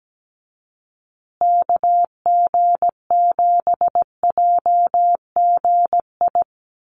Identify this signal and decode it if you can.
Morse code, heard as a keyed tone, about 17 wpm, 700 Hz. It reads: KG7JGI
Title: USA Call Signs Repeated 3x 17wpm